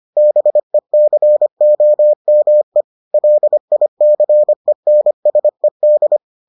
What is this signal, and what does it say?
Morse code, heard as a keyed tone, about 25 wpm, 600 Hz. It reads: BECOME LICENSED